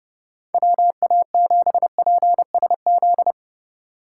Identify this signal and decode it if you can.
Morse code, heard as a keyed tone, about 30 wpm, 700 Hz. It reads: WA7PSZ